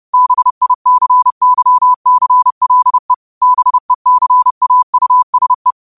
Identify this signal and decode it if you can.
Morse code, heard as a keyed tone, about 30 wpm, 1000 Hz. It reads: BICYCLE BECAUSE
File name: qotdfiles / 30wpm